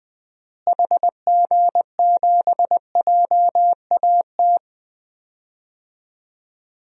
Morse code, heard as a keyed tone, about 20 wpm, 700 Hz. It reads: HG7JAT
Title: DXCC-100 Call Signs - 6 Characters - Repeated 2x - 20wpm